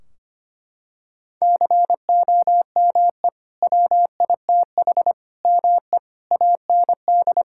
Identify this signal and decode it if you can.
Morse code, heard as a keyed tone, about 25 wpm, 700 Hz. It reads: COME WITH ME AND
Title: Sentences from Top 1600 Words 25wpm